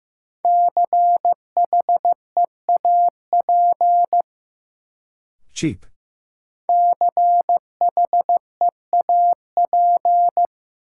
Morse code, heard as a keyed tone, about 15 wpm, 700 Hz. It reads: CHEAP CHEAP
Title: Sentences from Top 1200 Words - New Word Drill Alt - 15wpm